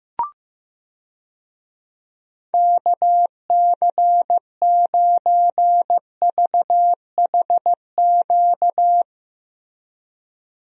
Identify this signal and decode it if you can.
Morse code, heard as a keyed tone, about 15 wpm, 700 Hz. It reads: KC9VHQ
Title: USA Call Signs - 6 Characters - Repeated 2x - 15wpm